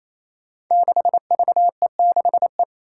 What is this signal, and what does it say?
Morse code, heard as a keyed tone, about 28 wpm, 700 Hz. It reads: 6VE6E
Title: Five Challenging Letter-Number Combinations 28wpm